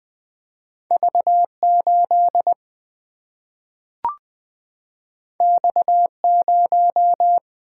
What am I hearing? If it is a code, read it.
Morse code, heard as a keyed tone, about 20 wpm, 700 Hz. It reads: V8 X0
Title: Single-Letter Single-Number 20wpm